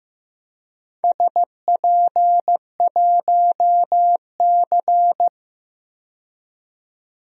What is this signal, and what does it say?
Morse code, heard as a keyed tone, about 15 wpm, 700 Hz. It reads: SP1C
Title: DXCC-100 Call Signs - 4 Characters - 15wpm